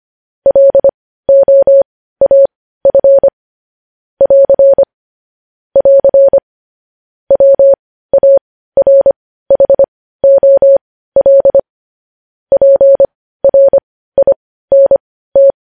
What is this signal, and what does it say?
Morse code, heard as a keyed tone, about 25 wpm, spaced as for 15 wpm, 550 Hz. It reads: LOAF <AR> <AR> WARHOL PRINT
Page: News headlines at 15